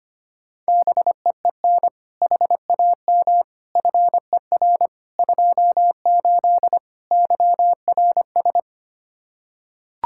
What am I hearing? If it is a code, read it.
Morse code, heard as a keyed tone, about 25 wpm, 700 Hz. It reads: BEEN HAM FER 28 YRS
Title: Practice QSO - Exchange 3 - More Info - 25wpm